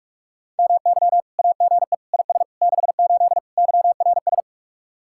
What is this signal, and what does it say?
Morse code, heard as a keyed tone, about 45 wpm, 700 Hz. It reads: MY AGE IS 68 YRS
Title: Speed-Racing Common QSO Phrases - 45wpm